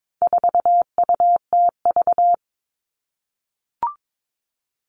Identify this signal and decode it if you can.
Morse code, heard as a keyed tone, about 22 wpm, 700 Hz. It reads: 4UTV
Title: All Cut Numbers - 4-Digit 22wpm